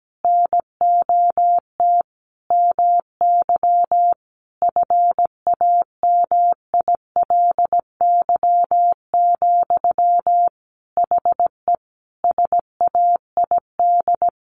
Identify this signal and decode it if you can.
Morse code, heard as a keyed tone, about 17 wpm, 700 Hz. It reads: NOT MY FAMILY, HE SAID
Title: Sentences from Top 1100 Words 17wpm